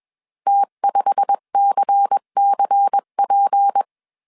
Morse code, heard as a keyed tone, about 21 wpm, 800 Hz. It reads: T5CCP
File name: CW / DET / 1 9 1500C S B 20wpm 2313